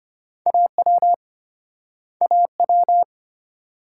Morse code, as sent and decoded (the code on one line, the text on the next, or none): .- .-- / .- .--
AW AW